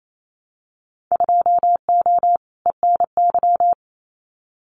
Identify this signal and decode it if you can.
Morse code, heard as a keyed tone, about 28 wpm, 700 Hz. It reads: 2O ENY
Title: Winter Field Day Exchanges 28wpm